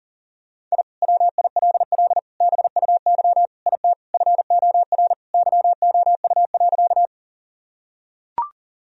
Morse code, heard as a keyed tone, about 40 wpm, 690 Hz. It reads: I WILL BUY IT FOR YOU.
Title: Sentences from Top 1000 Words - Review Part 4 - 40wpm